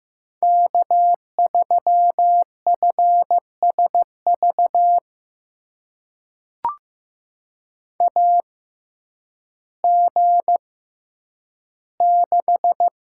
Morse code, heard as a keyed tone, about 15 wpm, spaced as for 9 wpm, 700 Hz. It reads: K3FSV AG6
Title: USA Call Signs - 6x Character Spacing - 15wpm